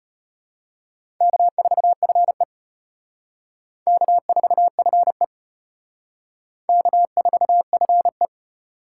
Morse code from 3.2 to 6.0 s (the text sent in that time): K4FE